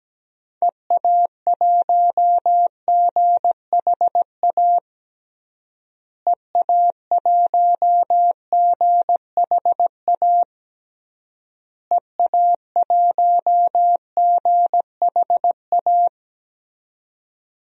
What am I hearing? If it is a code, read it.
Morse code, heard as a keyed tone, about 17 wpm, 700 Hz. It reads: EA1GHA EA1GHA EA1GHA
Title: DXCC-100 Call Signs - 6 Characters - Repeated 3x - 17wpm